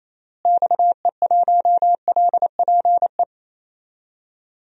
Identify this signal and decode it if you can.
Morse code, heard as a keyed tone, about 28 wpm, 700 Hz. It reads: XE1LPE
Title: DXCC-100 Call Signs - 6 Characters - Repeated 3x - 28wpm